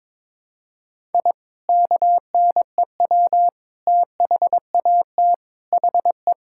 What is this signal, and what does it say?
Morse code, heard as a keyed tone, about 22 wpm, 700 Hz. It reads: I KNEW THAT HE
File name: practice / 22wpm / Sentences with 3-Word Phrases - 22wpm